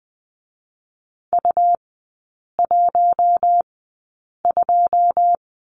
Morse code, heard as a keyed tone, about 20 wpm, spaced as for 8 wpm, 700 Hz. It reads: U12
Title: 127 Lesson 21e - Number 6 - Three Characters 20wpm